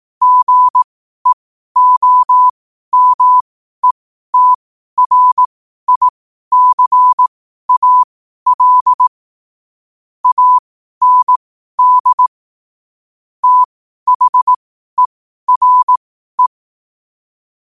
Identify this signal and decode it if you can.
Morse code, heard as a keyed tone, about 18 wpm, spaced as for 12 wpm, 1000 Hz. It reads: GEOMETRICAL AND THERE
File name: qotdfiles / 12wpm